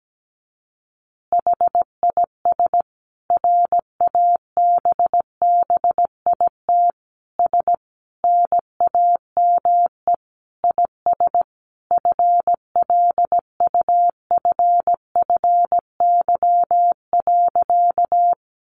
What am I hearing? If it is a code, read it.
Morse code, heard as a keyed tone, about 17 wpm, 700 Hz. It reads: HIS RABBIT S NAME IS FLUFFY.
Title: Easy Phrases 17wpm